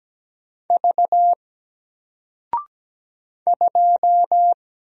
Morse code, heard as a keyed tone, about 17 wpm, 700 Hz. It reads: V 2
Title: Single Character (Letter:Number:Punctuation:Prosign) Alt 17wpm